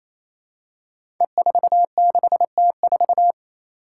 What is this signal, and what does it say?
Morse code, heard as a keyed tone, about 28 wpm, 700 Hz. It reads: E46T4
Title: Five Challenging Letter-Number Combinations 28wpm